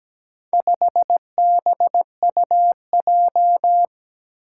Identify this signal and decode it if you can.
Morse code, heard as a keyed tone, about 17 wpm, 700 Hz. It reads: 5BUJ
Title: Four Challenging Letter-Number Combinations 17wpm